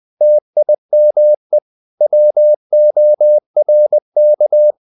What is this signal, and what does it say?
Morse code, heard as a keyed tone, about 20 wpm, 600 Hz. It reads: TIME WORK
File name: CW / Practice / 500MostCommonEnglishWords20